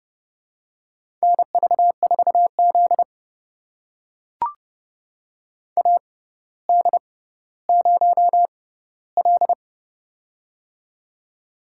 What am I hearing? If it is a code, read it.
Morse code, heard as a keyed tone, about 30 wpm, spaced as for 15 wpm, 700 Hz. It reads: NV4Z AD0L
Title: USA Call Signs - 4 Characters - 6x Character Spacing - 30wpm